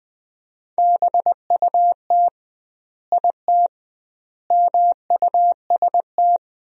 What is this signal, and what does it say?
Morse code, heard as a keyed tone, about 20 wpm, 700 Hz. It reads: BUT IT MUST
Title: 2x Word-Spacing - Sentences from Top 500 Words - 20wpm